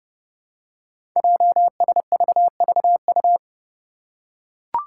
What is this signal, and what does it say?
Morse code, heard as a keyed tone, about 30 wpm, 700 Hz. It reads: JSVVU
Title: Five Challenging Letter-Number Combinations 30wpm